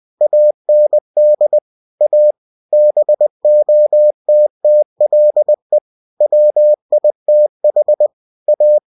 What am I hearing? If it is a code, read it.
Morse code, heard as a keyed tone, about 20 wpm, 600 Hz. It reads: AND A BOTTLE WITH A
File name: cw-books / winnie-the-pooh / winnie-20wpm-page-0092